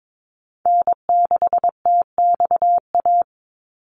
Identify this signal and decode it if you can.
Morse code, heard as a keyed tone, about 22 wpm, 700 Hz. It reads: N6TXA
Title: USA Call Signs 22wpm